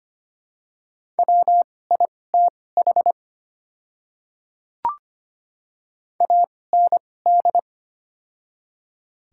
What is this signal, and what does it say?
Morse code, heard as a keyed tone, about 25 wpm, spaced as for 18 wpm, 700 Hz. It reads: WITH AND
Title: 2x Character Spacing - Top 100 Words - 25wpm